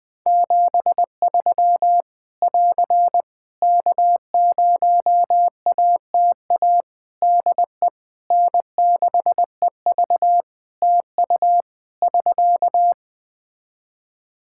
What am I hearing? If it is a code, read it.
Morse code, heard as a keyed tone, about 20 wpm, 700 Hz. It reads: 73 <AR> K0ATA DE N6EV TU <SK>
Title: Practice QSO - Exchange 4 - Ending - 20wpm